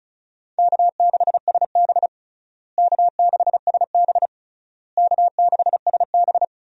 Morse code, heard as a keyed tone, about 36 wpm, 700 Hz. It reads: K6SB K6SB K6SB
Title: USA Call Signs - 4 Characters - Repeated 3x - 35wpm